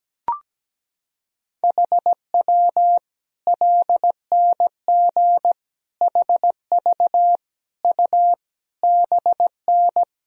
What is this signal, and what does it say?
Morse code, heard as a keyed tone, about 17 wpm, 700 Hz. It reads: HW LNG HV U BN
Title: Sentences from Top 200 Words and Abbreviations 17wpm